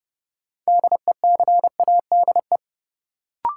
Morse code, heard as a keyed tone, about 30 wpm, 700 Hz. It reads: DECADE
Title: Sets of 5 Words - New Word Drill - 30wpm